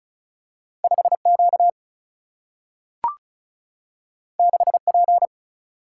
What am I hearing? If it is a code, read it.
Morse code, heard as a keyed tone, about 36 wpm, 700 Hz. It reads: 5 Q 6 P
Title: Single-Letter Single-Number 35wpm